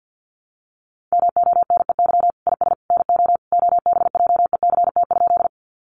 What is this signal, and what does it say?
Morse code, heard as a keyed tone, about 50 wpm, 700 Hz. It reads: MONEY IS NO OBJECT?
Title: Sentences from Top 500 Words 50wpm